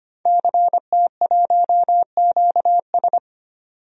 Morse code, heard as a keyed tone, about 25 wpm, 700 Hz. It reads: CT1QS
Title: DXCC-100 Call Signs - 5 Characters - 25wpm